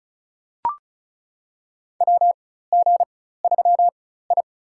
Morse code, heard as a keyed tone, about 36 wpm, spaced as for 16 wpm, 700 Hz. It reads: WG3I
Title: USA Call Signs - 4 Characters - 4x Character Spacing - 35wpm